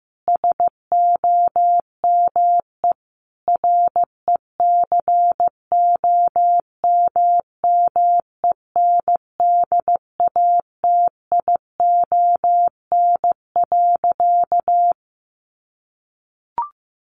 Morse code, heard as a keyed tone, about 14 wpm, 700 Hz. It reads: SOME RECOMMENDATION.
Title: Sentences from Top 1400 Words 15wpm